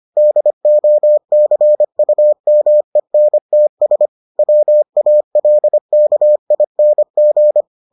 Morse code, heard as a keyed tone, about 25 wpm, 600 Hz. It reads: DOCUMENTS WALKING